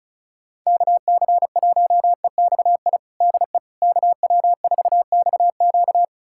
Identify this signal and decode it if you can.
Morse code, heard as a keyed tone, about 34 wpm, 700 Hz. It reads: KC1EXI DE KW4XQ
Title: USA Callsign DE Callsign 35wpm